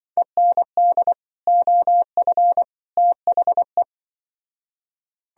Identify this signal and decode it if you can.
Morse code, heard as a keyed tone, about 25 wpm, 700 Hz. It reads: END OF THE
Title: Speed-Racing - Sets of 3 Words 20wpm